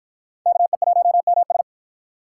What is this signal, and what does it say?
Morse code, heard as a keyed tone, about 51 wpm, 700 Hz. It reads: KE1MS